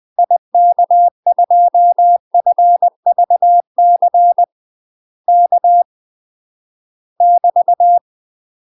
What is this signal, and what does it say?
Morse code, heard as a keyed tone, about 20 wpm, 700 Hz. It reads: IK2FVC K <BT>
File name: UK MASTER QSOs TRAINING / 20WPM / UK MASTER QSOs 0017